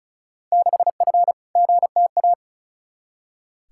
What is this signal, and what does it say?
Morse code, heard as a keyed tone, about 34 wpm, 700 Hz. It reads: 6F GTA
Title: Field Day Exchanges Encore - 35wpm